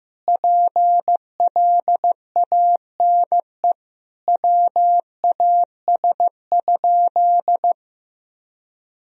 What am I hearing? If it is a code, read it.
Morse code, heard as a keyed tone, about 15 wpm, 700 Hz. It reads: PLANE WAS?
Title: Sentences from Top 1000 Words - Review Part 3 - 15wpm